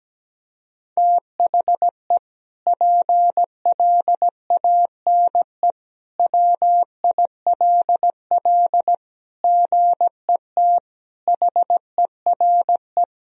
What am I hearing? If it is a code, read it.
Morse code, heard as a keyed tone, about 17 wpm, 700 Hz. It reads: THE PLANE WILL GET HERE
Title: Sentences from Top 1000 Words - Review Part 3 - 17wpm